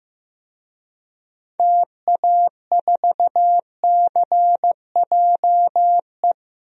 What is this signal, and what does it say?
Morse code, heard as a keyed tone, about 15 wpm, 700 Hz. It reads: TA4CJE
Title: DXCC-100 Call Signs - 6 Characters - 15wpm